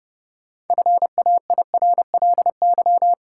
Morse code, heard as a keyed tone, about 30 wpm, 700 Hz. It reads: FAIRLY
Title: Sentences from Top 1100 Words - New Word Drill Alt - 30wpm